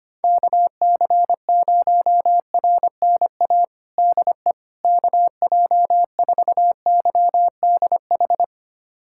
Morse code, heard as a keyed tone, about 25 wpm, 700 Hz. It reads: KC0RNA DE KJ4YDH
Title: USA Callsign DE Callsign 25wpm